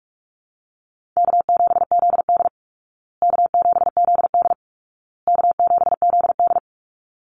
Morse code, heard as a keyed tone, about 44 wpm, 700 Hz. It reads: K7ZD K7ZD K7ZD
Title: Easy Call Signs Repeated 3x - 45wpm